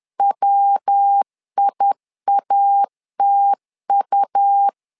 Morse code, heard as a keyed tone, about 10 wpm, 800 Hz. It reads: WIATU
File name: LEZIONI / 4 Q 10wpm 2210